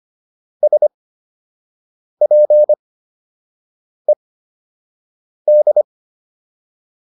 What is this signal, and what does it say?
Morse code, heard as a keyed tone, about 25 wpm, spaced as for 6 wpm, 600 Hz. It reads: SPED